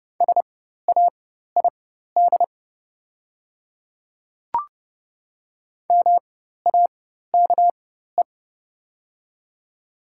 Morse code, heard as a keyed tone, about 30 wpm, spaced as for 14 wpm, 690 Hz. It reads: SAID MAKE
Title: 4x Character Spacing - Top 200 Words - 30wpm